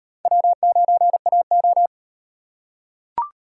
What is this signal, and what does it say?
Morse code, heard as a keyed tone, about 38 wpm, 700 Hz. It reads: W9AO